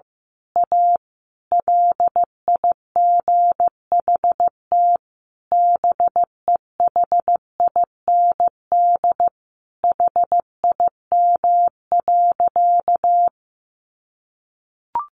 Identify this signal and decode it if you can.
Morse code, heard as a keyed tone, about 15 wpm, 700 Hz. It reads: A LIGHT BEHIND HIM.
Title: Sentences from Top 1000 Words 15wpm